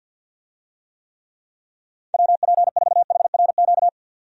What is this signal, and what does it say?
Morse code, heard as a keyed tone, about 48 wpm, 700 Hz. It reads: WP4HRQ
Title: USA Call Signs - 6 Characters - 50wpm